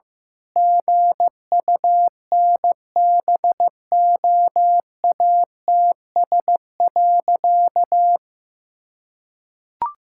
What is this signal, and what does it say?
Morse code, heard as a keyed tone, about 16 wpm, 700 Hz. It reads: GUNBOATS.
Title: Pangram Challenge v2 - 15wpm